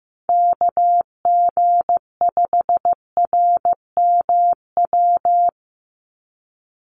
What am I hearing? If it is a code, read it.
Morse code, heard as a keyed tone, about 15 wpm, 700 Hz. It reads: KG5RMW
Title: USA Call Signs 15wpm